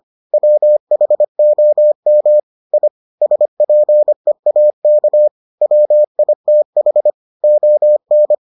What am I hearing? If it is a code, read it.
Morse code, heard as a keyed tone, about 25 wpm, 600 Hz. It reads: WHOM I SPEAK WITH ON